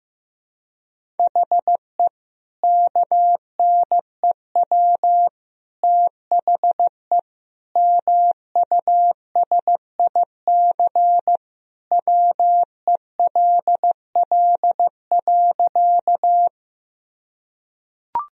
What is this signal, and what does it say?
Morse code, heard as a keyed tone, about 16 wpm, 700 Hz. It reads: HE KNEW THE MUSIC WELL.
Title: Sentences from Top 1000 Words - Review Part 1 - 15wpm